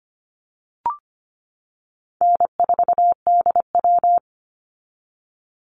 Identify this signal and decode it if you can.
Morse code, heard as a keyed tone, about 25 wpm, 700 Hz. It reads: N4DW
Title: Call Signs - 25wpm